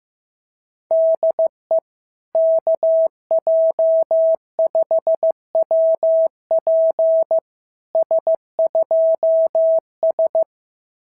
Morse code, heard as a keyed tone, about 15 wpm, 650 Hz. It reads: DE KJ5WP S2S
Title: SOTA QSO Practice 15wpm